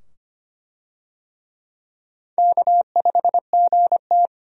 Morse code, heard as a keyed tone, about 25 wpm, 700 Hz. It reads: K5GT
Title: USA Call Signs - 4 Characters - 25wpm